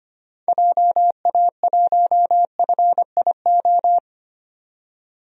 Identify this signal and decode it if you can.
Morse code, heard as a keyed tone, about 25 wpm, 690 Hz. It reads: JA1FIO
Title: DXCC-100 Call Signs - 6 Characters - 25wpm